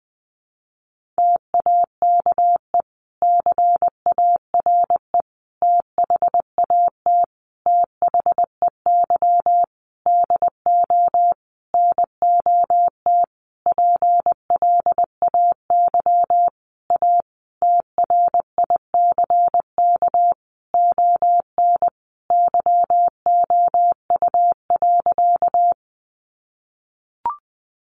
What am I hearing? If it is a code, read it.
Morse code, heard as a keyed tone, about 20 wpm, 700 Hz. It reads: TAKE CARE THAT THEY DO NOT PLAY A TRICK ON YOU.
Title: Sentences from Top 1500 Words 20wpm